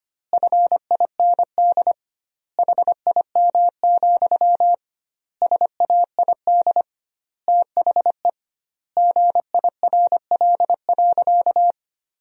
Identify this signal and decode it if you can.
Morse code, heard as a keyed tone, about 25 wpm, 700 Hz. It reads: FIND HIM, SAID THE GIRL.
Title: 2x Word-Spacing - Sentences from Top 400 Words - 25wpm